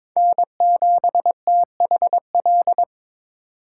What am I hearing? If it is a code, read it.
Morse code, heard as a keyed tone, about 27 wpm, 700 Hz. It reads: N7THL
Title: Speed Racing - USA 1x3 Call Signs 22wpm